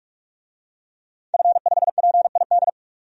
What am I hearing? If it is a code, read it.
Morse code, heard as a keyed tone, about 46 wpm, 700 Hz. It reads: F5PID
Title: DXCC-100 Call Signs - 5 Characters - 45wpm